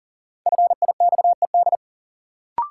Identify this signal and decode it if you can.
Morse code, heard as a keyed tone, about 40 wpm, 700 Hz. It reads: FIXED
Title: Sentences from Top 1100 Words - New Word Drill - 40wpm